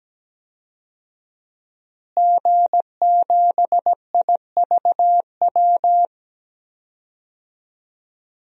Morse code, heard as a keyed tone, about 17 wpm, 700 Hz. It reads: G7IVW
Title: DXCC-100 Call Signs - 5 Characters - Repeated 3x - 17wpm